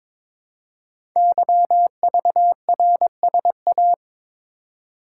Morse code, heard as a keyed tone, about 22 wpm, 700 Hz. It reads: YVRSA
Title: Code Groups - 5 Letters - 22wpm